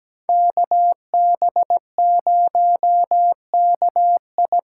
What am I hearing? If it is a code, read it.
Morse code, heard as a keyed tone, about 20 wpm, 700 Hz. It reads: KB0KI
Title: Speed Racing - USA Call Signs 17wpm